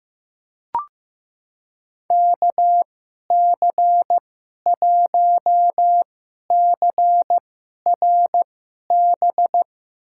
Morse code, heard as a keyed tone, about 15 wpm, spaced as for 11 wpm, 700 Hz. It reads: KC1CRB
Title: USA Call Signs - 2x Character Spacing - 15wpm